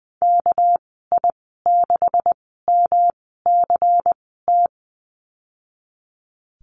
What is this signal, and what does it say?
Morse code, heard as a keyed tone, about 19 wpm, spaced as for 14 wpm, 700 Hz. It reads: KI6MCT
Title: 200 Lesson 32g - Letter k - Call Signs 20wpm